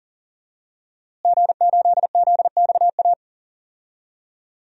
Morse code, heard as a keyed tone, about 38 wpm, 700 Hz. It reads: G8ZXA
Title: DXCC-100 Call Signs - 5 Characters - Repeated 3x - 40wpm